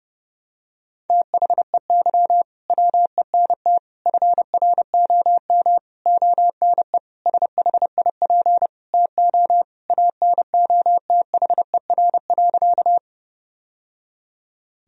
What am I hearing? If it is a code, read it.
Morse code, heard as a keyed tone, about 30 wpm, 700 Hz. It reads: THEY WENT FROM ONE SHIP TO ANOTHER.
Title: Sentences from Top 1000 Words - Review Part 2 - 30wpm